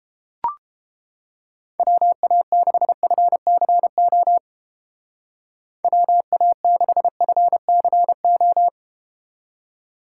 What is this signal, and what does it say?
Morse code, heard as a keyed tone, about 30 wpm, 700 Hz. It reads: WA6FCO WA6FCO